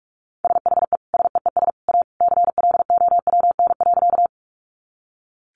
Morse code, heard as a keyed tone, about 44 wpm, 700 Hz. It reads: SHE SEES A CLOWN.
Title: Easy Phrases 45wpm